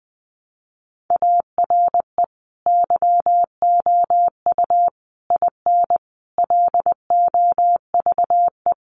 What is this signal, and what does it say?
Morse code, heard as a keyed tone, about 20 wpm, 700 Hz. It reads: ARE YOU IN LOVE
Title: Sentences from Top 400 Words 20wpm